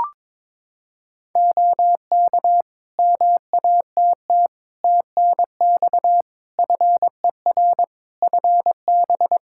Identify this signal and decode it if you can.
Morse code, heard as a keyed tone, about 21 wpm, 700 Hz. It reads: OK MATT TNX FER FB
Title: Practice QSO - Exchange 4 - Ending - 22wpm